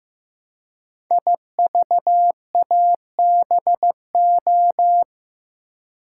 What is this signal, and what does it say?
Morse code, heard as a keyed tone, about 15 wpm, 700 Hz. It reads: IVABO
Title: Code Groups - 5 Letters - 15wpm